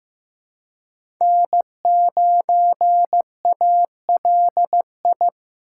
Morse code, heard as a keyed tone, about 15 wpm, 700 Hz. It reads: N9ALI
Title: USA Call Signs - 5 Characters - 8x Character Spacing - 15wpm